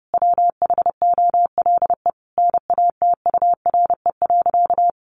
Morse code, heard as a keyed tone, about 30 wpm, 700 Hz. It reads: WHOLE NATURE.
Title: Sentences from Top 1500 Words 30wpm